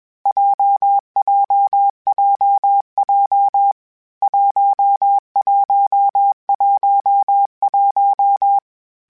Morse code, heard as a keyed tone, about 21 wpm, 800 Hz. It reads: JJJJ 1111
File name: Basic Sets / Basic Letter Groups Collective7 / JZ17@20